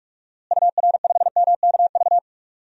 Morse code, heard as a keyed tone, about 43 wpm, 700 Hz. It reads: UR5MKV
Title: DXCC-100 Call Signs - 6 Characters - Repeated 2x - 45wpm